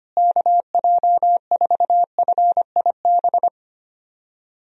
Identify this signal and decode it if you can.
Morse code, heard as a keyed tone, about 25 wpm, 700 Hz. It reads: KJ4FIB
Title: USA Call Signs - 6 Characters - Repeated 3x - 25wpm